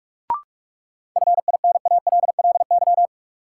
Morse code, heard as a keyed tone, about 46 wpm, 700 Hz. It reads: FINALLY
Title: Sentences from Top 1000 Words - New Word Drill - 45wpm